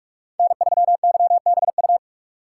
Morse code, heard as a keyed tone, about 43 wpm, 700 Hz. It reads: N3YBU
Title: USA Call Signs - 5 Characters - 4x Character Spacing - 45wpm